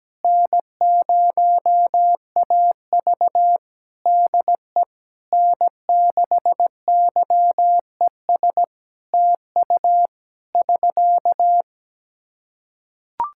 Morse code, heard as a keyed tone, about 17 wpm, 700 Hz. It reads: N0AV DE N6YES TU <SK>
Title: Practice QSO - Exchange 4 - Ending - 17wpm